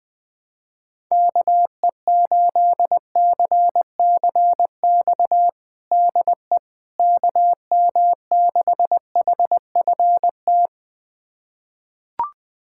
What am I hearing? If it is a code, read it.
Morse code, heard as a keyed tone, about 21 wpm, 700 Hz. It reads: KE8CCX DE KM6HFT
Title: USA Callsign DE Callsign - 6 Characters - 20wpm